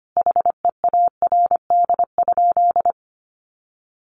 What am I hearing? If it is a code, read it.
Morse code, heard as a keyed tone, about 25 wpm, 700 Hz. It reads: HEARD?
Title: Sentences from Top 400 Words 25wpm